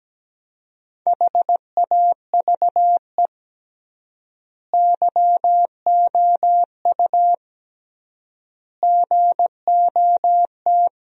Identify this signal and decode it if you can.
Morse code, heard as a keyed tone, about 17 wpm, 700 Hz. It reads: HAVE YOU GOT
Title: 3x Word-Spacing - Sentences from Top 300 Words - 17wpm